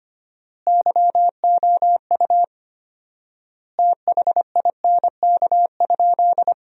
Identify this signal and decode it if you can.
Morse code, heard as a keyed tone, about 25 wpm, 700 Hz. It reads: YOU THINK?
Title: 4x Word-Spacing - Sentences from Top 200 Words - 25wpm